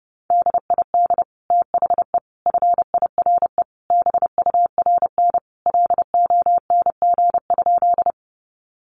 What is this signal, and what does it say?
Morse code, heard as a keyed tone, about 30 wpm, 700 Hz. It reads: DID THE FIRE BURN LONG?
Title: Sentences from Top 1300 Words 30wpm